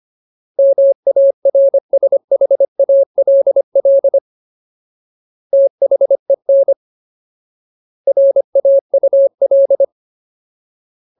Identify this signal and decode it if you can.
Morse code, heard as a keyed tone, about 25 wpm, 550 Hz. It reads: MARSHALL THEN RAUL
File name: Ultimate Word List / 25WPM Ultimate